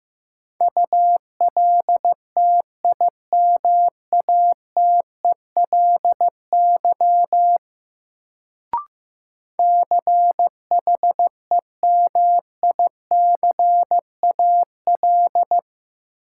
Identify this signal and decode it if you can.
Morse code, heard as a keyed tone, about 16 wpm, 700 Hz. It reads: ULTIMATELY CHEMICAL
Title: Sentences from Top 1200 Words - New Word Drill - 15wpm